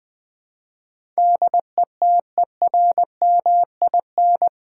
Morse code, heard as a keyed tone, about 20 wpm, 700 Hz. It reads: DETERMIN
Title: Build A Word - 4 Letters At A Time - 10-Letter Words - 20wpm